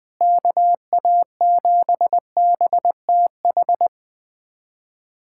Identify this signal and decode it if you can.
Morse code, heard as a keyed tone, about 21 wpm, 700 Hz. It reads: KA7BTH
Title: USA Callsign DE Callsign 20wpm